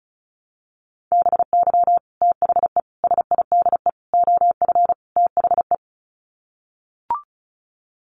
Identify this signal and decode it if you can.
Morse code, heard as a keyed tone, about 34 wpm, 700 Hz. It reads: BY THE SIDE OF THE
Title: Sets of 5 Words Encore2 35wpm